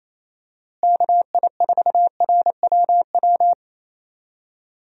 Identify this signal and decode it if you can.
Morse code, heard as a keyed tone, about 28 wpm, 700 Hz. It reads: KI4RWW
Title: USA Call Signs - 6 Characters - Repeated 2x - 28wpm